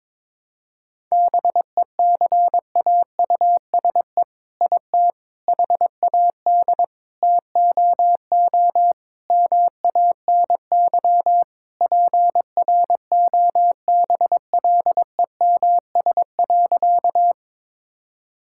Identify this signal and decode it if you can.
Morse code, heard as a keyed tone, about 22 wpm, 700 Hz. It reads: BECAUSE IT HAD TOO MANY PROBLEMS.
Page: morse-code-ninja-practice-22wpm